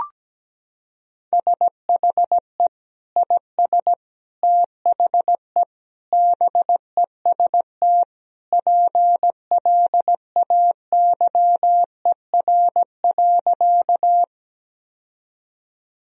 Morse code, heard as a keyed tone, about 17 wpm, 700 Hz. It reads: SHE IS THE BEST PLAYER.
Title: Sentences from Top 1000 Words - Review Part 4 - 17wpm